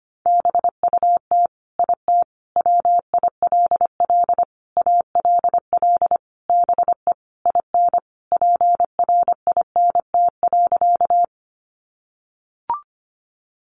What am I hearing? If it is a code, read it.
Morse code, heard as a keyed tone, about 25 wpm, 700 Hz. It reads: BUT IT WILL ALL BE IN PRINT.
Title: Sentences from Top 1400 Words 25wpm